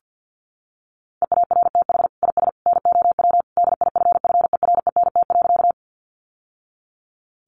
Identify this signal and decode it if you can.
Morse code, heard as a keyed tone, about 50 wpm, 700 Hz. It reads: EARTH IS NOW DIFFERENT.
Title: Sentences from Top 600 Words 50wpm